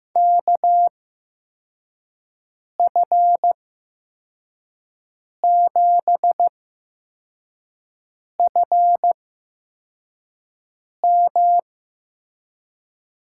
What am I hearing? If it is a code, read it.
Morse code, heard as a keyed tone, about 15 wpm, spaced as for 4 wpm, 700 Hz. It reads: KF7FM
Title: USA Call Signs - 6 Characters - 8x Character Spacing - 15wpm